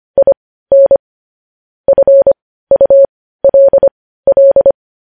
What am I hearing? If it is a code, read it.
Morse code, heard as a keyed tone, about 25 wpm, spaced as for 15 wpm, 550 Hz. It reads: IN FULL
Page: news headlines in morse code at 15 wpm